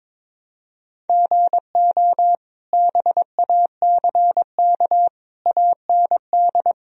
Morse code, heard as a keyed tone, about 21 wpm, 700 Hz. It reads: GO BACK AND
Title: Sentences from Top 1000 Words - Review Part 1 - 22wpm